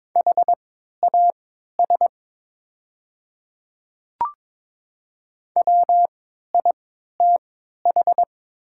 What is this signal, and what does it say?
Morse code, heard as a keyed tone, about 22 wpm, spaced as for 12 wpm, 700 Hz. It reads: HAS WITH
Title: 3x Character Spacing - Top 100 Words - 22wpm